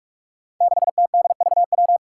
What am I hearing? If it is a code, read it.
Morse code, heard as a keyed tone, about 41 wpm, 700 Hz. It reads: 6TDVW
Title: Five Challenging Letter-Number Combinations 45wpm